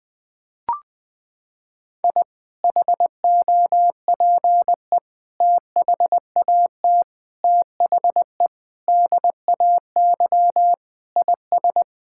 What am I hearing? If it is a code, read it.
Morse code, heard as a keyed tone, about 20 wpm, 700 Hz. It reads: I HOPE THAT THE DAY IS
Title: Sentences with 2-Word Phrases - 20wpm